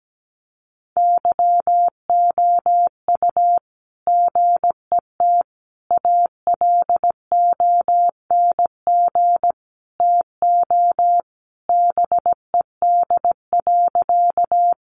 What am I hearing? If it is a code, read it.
Morse code, heard as a keyed tone, about 17 wpm, 700 Hz. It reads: YOU GET ALONG TO BED.
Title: Sentences from Top 700 Words 17wpm